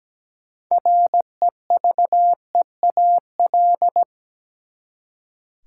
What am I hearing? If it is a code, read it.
Morse code, heard as a keyed tone, about 17 wpm, 700 Hz. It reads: REVEAL
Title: Six Letter Words - 17wpm